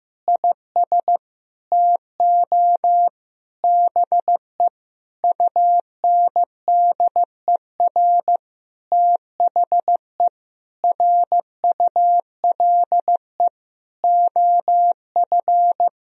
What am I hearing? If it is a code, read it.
Morse code, heard as a keyed tone, about 15 wpm, 700 Hz. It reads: IS TO BE UNDER THE RULE OF
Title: Sentences from Top 1000 Words - Review Part 3 - 15wpm